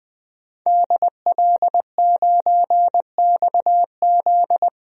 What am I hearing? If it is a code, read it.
Morse code, heard as a keyed tone, about 20 wpm, 700 Hz. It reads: DL9XZ
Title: DXCC-100 Call Signs - 5 Characters - 20wpm